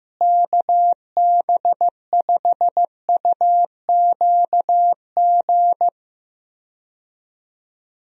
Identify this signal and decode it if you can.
Morse code, heard as a keyed tone, about 15 wpm, 700 Hz. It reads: KB5UQG
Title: USA Call Signs - 6 Characters - Repeated 3x - 15wpm